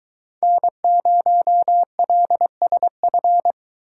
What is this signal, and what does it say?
Morse code, heard as a keyed tone, about 22 wpm, 700 Hz. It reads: N0LSF